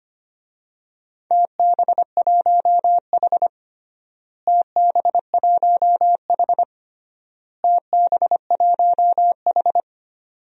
Morse code, heard as a keyed tone, about 25 wpm, 700 Hz. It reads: TB1H TB1H TB1H
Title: DXCC-100 Call Signs - 4 Characters - Repeated 3x - 25wpm